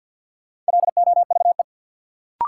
Morse code, heard as a keyed tone, about 52 wpm, 700 Hz. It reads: LOVE
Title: Sentences from Top 400 Words - New Word Drill - 50wpm